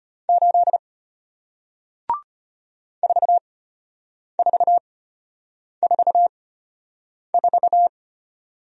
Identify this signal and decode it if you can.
Morse code, heard as a keyed tone, about 33 wpm, spaced as for 8 wpm, 700 Hz. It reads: 8 4444